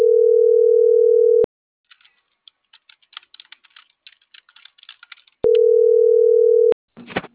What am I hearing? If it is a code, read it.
Ringback tone (North America), 440 + 480 Hz.